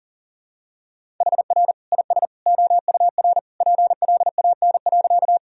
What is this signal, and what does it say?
Morse code, heard as a keyed tone, about 37 wpm, 700 Hz. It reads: HR IS OUR PLAN.
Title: Sentences from Top 500 Words and Abbreviations 40wpm